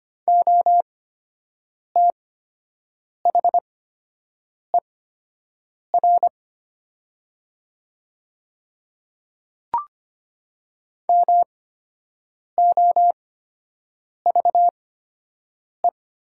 Morse code, heard as a keyed tone, about 25 wpm, spaced as for 7 wpm, 700 Hz. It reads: OTHER MOVE
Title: 8x Character Spacing - Top 200 Words - 25wpm